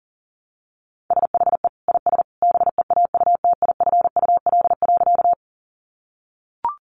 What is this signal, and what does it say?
Morse code, heard as a keyed tone, about 40 wpm, 700 Hz. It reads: SHE IS BEAUTIFUL.
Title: Sentences from Top 900 Words 40wpm